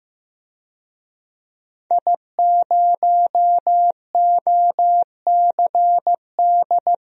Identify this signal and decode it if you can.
Morse code, heard as a keyed tone, about 16 wpm, 700 Hz. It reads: I0OCD
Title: DXCC-100 Call Signs - 5 Characters - Repeated 2x - 15wpm